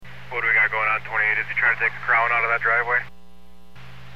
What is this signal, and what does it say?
Emergency Alert System (SAME) tones, heard as harsh buzzing bursts, with speech. It is heard from 0.3 to 3.1 s.